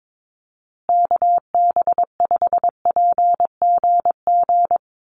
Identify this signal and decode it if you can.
Morse code, heard as a keyed tone, about 22 wpm, 700 Hz. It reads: KB5PGG
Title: USA Call Signs Repeated 2x 22wpm